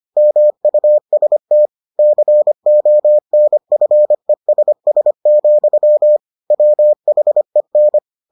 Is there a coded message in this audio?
Yes: Morse code from 0.2 to 8.0 s, heard as a keyed tone, about 25 wpm, 600 Hz. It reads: MUST CONFESS, WHEN